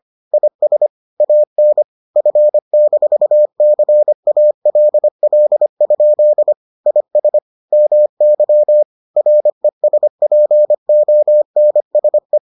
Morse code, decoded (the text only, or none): IS AN F-CALL? IS MY RESPONSE